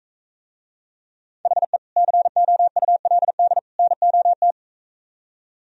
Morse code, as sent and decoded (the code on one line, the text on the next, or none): .... / -.-. --- ..- .-.. -.. / -. --- -
HE COULD NOT